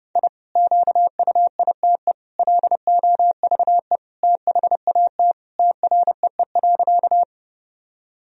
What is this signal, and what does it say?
Morse code, heard as a keyed tone, about 30 wpm, 700 Hz. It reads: I QUITE LOVE THAT TREE.
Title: Sentences from Top 1000 Words - Review Part 1 - 30wpm